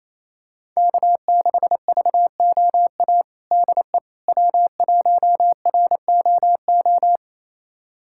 Morse code, heard as a keyed tone, about 27 wpm, 700 Hz. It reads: K6VOA DE W1ROO
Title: USA Callsign DE Callsign - 5 Characters - 28wpm